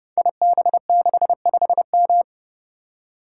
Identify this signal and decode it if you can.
Morse code, heard as a keyed tone, about 30 wpm, 700 Hz. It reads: IB65M
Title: Code Groups - 5 Letter:Numbers - 30wpm